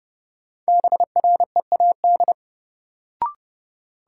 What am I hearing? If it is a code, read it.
Morse code, heard as a keyed tone, about 32 wpm, 700 Hz. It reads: BREAD
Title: Sentences from Top 1600 Words - New Word Drill - 30wpm